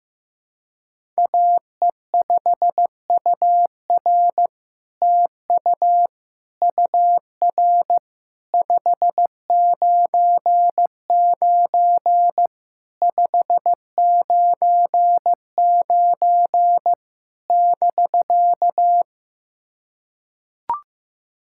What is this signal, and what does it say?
Morse code, heard as a keyed tone, about 15 wpm, 700 Hz. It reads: AE5UR TU UR 599 599 <BK>
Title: SOTA QSO Practice 15wpm